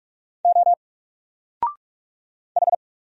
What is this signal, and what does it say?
Morse code, heard as a keyed tone, about 46 wpm, 700 Hz. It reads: O H
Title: Single Letter-Number Alt 45wpm